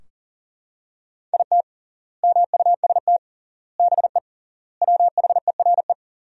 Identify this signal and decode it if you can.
Morse code, heard as a keyed tone, about 40 wpm, 700 Hz. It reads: IT MUST BE WHERE
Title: 3x Word-Spacing - Sentences from Top 200 Words - 40wpm